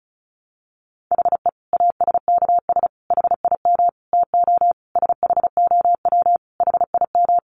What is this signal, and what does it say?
Morse code, heard as a keyed tone, about 34 wpm, 700 Hz. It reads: HE ASKS HIM TO SHOW HIM
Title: Easy Phrases 35wpm